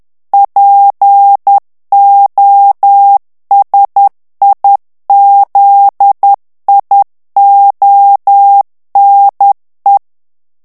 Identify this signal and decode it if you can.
Morse code, heard as a keyed tone, about 10 wpm, 800 Hz. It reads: POSIZIONE
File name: ITALIEN texte clair 2 vitesse 10 mots minute